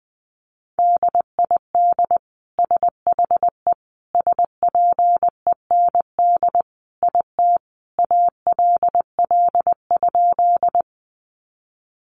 Morse code, heard as a keyed tone, about 20 wpm, 700 Hz. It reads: DID SHE SPEND IT ALL?
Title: Sentences from Top 800 Words 20wpm